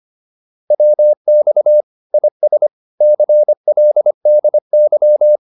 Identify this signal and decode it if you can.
Morse code, heard as a keyed tone, about 25 wpm, 600 Hz. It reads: WX IS CLDY
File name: PR302 25